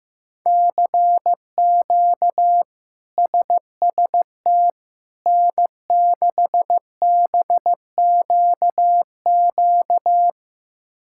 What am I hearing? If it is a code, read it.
Morse code, heard as a keyed tone, about 16 wpm, 700 Hz. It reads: CQ SST N6BQQ
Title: K1USN SST QSO Practice - 15wpm